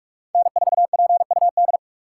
Morse code, heard as a keyed tone, about 41 wpm, 700 Hz. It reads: N4PUD
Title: USA Call Signs - 5 Characters - 2x Character Spacing - 45wpm